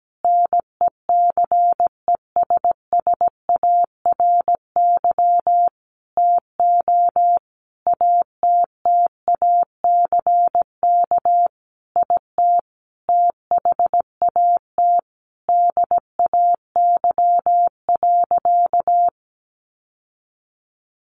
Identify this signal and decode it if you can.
Morse code, heard as a keyed tone, about 17 wpm, 700 Hz. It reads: NECESSARY TO ATTACK IT THAT DAY.
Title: Sentences from Top 1500 Words 17wpm